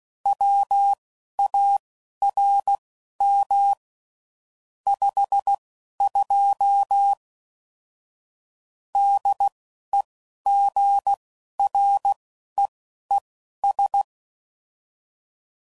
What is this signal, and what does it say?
Morse code, heard as a keyed tone, about 16 wpm, spaced as for 11 wpm, 800 Hz. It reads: WARM 52 DEGREES
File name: CW Code Groups @10WPM / Weather Sets / Weather 4 10